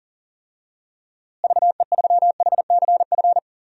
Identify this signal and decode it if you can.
Morse code, heard as a keyed tone, about 37 wpm, 700 Hz. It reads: VE3HCF
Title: DXCC-100 Call Signs - 6 Characters - Repeated 3x - 40wpm